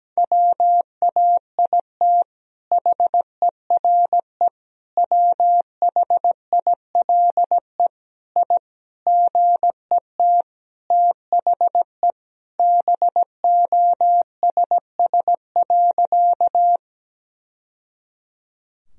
Morse code, heard as a keyed tone, about 17 wpm, 700 Hz. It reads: WAIT HERE WHILE I GET THE BOSS.
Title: Sentences from Top 1000 Words - Review Part 4 - 17wpm